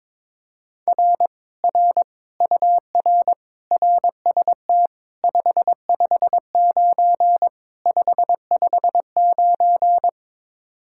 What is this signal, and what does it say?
Morse code, heard as a keyed tone, about 22 wpm, 700 Hz. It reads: R R UR RST 559 559
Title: Speed-Racing Common QSO Phrases - 22wpm